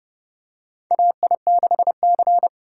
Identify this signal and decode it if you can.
Morse code, heard as a keyed tone, about 30 wpm, 700 Hz. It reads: AI6C
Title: USA Call Signs - 4 Characters - 3x Character Spacing - 30wpm